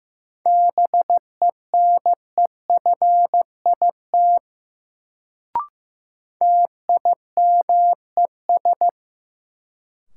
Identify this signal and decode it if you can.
Morse code, heard as a keyed tone, about 15 wpm, 700 Hz. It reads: BENEFIT TIMES
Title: Sentences from Top 1000 Words - Review Part 6 - New Word Drill - 15wpm